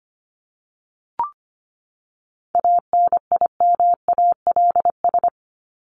Morse code, heard as a keyed tone, about 25 wpm, 700 Hz. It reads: ANIMALS
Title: ES:S Suffix - Top 100 Words with Base Words 25wpm